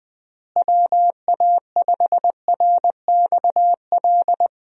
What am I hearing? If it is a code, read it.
Morse code, heard as a keyed tone, about 20 wpm, 700 Hz. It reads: WA5RXL
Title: USA Call Signs - 6 Characters - Repeated 3x - 20wpm